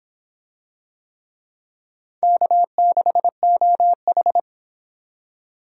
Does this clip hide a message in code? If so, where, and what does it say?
Yes: Morse code from 2.2 to 4.4 s, heard as a keyed tone, about 26 wpm, 700 Hz. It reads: K6OH